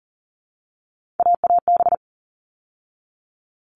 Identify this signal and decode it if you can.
Morse code, heard as a keyed tone, about 40 wpm, 700 Hz. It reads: AAB
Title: All Cut Numbers - 3-Digit 40wpm